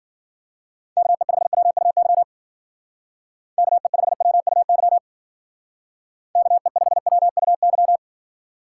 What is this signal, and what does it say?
Morse code, heard as a keyed tone, about 50 wpm, 700 Hz. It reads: KE5WUY KE5WUY KE5WUY
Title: Speed Racing - USA 2x3 Call Signs 40wpm